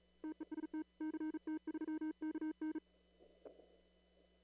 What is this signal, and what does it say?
Morse code, heard as a keyed tone, about 38 wpm, 310 Hz. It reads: TEST CT3KN
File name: CW / kt5j wpx cw-op-k5pi